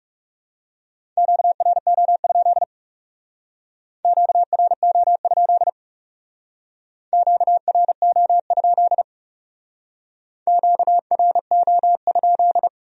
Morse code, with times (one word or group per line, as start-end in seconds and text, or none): 1.2-2.6 QRO?
4.0-5.7 QRO?
7.1-9.0 QRO?
10.5-12.7 QRO?